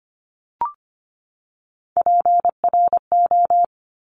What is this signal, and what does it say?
Morse code, heard as a keyed tone, about 25 wpm, 700 Hz. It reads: PRO
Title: Top Three Letter Combinations in Words 25wpm